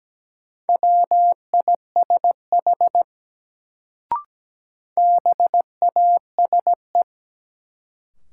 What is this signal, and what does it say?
Morse code, heard as a keyed tone, about 17 wpm, 700 Hz. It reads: WISH BASE
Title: Sentences from Top 1600 Words - New Word Drill Alt - 17wpm